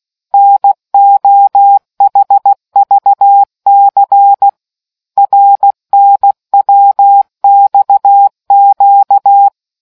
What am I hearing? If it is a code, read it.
Morse code, heard as a keyed tone, about 16 wpm, 800 Hz. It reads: NOHVC RNWXQ